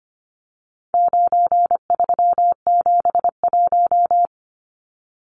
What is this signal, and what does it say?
Morse code, heard as a keyed tone, about 25 wpm, 700 Hz. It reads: 9371
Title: Numbers - 4-Digit 25wpm